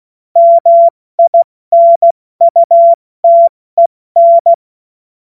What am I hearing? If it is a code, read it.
Morse code, heard as a keyed tone, about 16 wpm, 680 Hz. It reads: MINUTEN
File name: Morsekurs / Witze / Witz 22. August 2023 70 F80